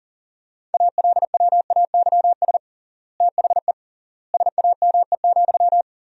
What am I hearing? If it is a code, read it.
Morse code, heard as a keyed tone, about 37 wpm, 700 Hz. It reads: ALWAYS THE SAME,
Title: 3x Word-Spacing - Sentences from Top 300 Words - 40wpm